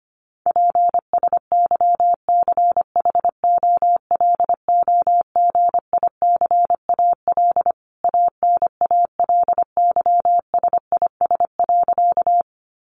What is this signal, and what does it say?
Morse code, heard as a keyed tone, about 25 wpm, 700 Hz. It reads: PSYCHOLOGICAL ANALYSIS.
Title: Sentences from Top 1400 Words 25wpm